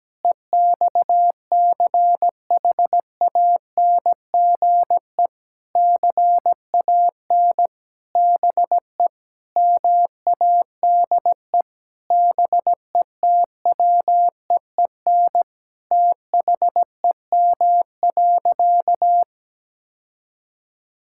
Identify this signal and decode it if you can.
Morse code, heard as a keyed tone, about 17 wpm, 700 Hz. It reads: EXCHANGE CAN BE MADE BETWEEN THEM.
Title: Sentences from Top 1000 Words - Review Part 6 - 17wpm